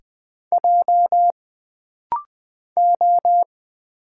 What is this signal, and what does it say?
Morse code, heard as a keyed tone, about 20 wpm, 700 Hz. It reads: J O
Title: Single Letter-Number Alt - Rapid-Fire 20wpm